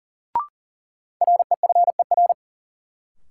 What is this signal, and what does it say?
Morse code, heard as a keyed tone, about 38 wpm, 700 Hz. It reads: REFER
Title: Sentences from Top 1000 Words - New Word Drill - 40wpm